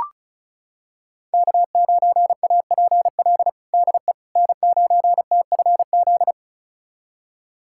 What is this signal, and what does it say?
Morse code, heard as a keyed tone, about 36 wpm, 700 Hz. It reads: K9APL DE N9TFZ
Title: USA Callsign DE Callsign - 5 Characters - 35wpm